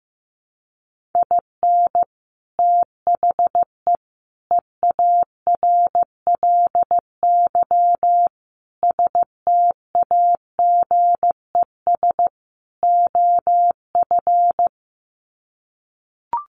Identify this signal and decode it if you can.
Morse code, heard as a keyed tone, about 15 wpm, 700 Hz. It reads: IN THE EARLY STAGES OF
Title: Sets of 5 Words Encore 15wpm